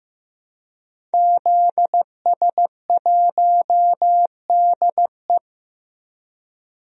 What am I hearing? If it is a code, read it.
Morse code, heard as a keyed tone, about 15 wpm, 700 Hz. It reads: ZS1DE
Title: DXCC-100 Call Signs - 5 Characters - 15wpm